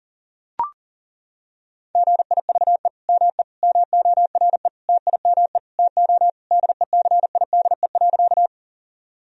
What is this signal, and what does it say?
Morse code, heard as a keyed tone, about 40 wpm, 700 Hz. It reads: GIVE ME MORE TIME TO DECIDE.
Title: Sentences from Top 1000 Words - Review Part 3 - 40wpm